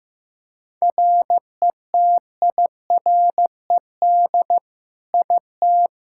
Morse code, heard as a keyed tone, about 15 wpm, 700 Hz. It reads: RETIRED IT
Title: Practice QSO - Exchange 3 - More Info - 15wpm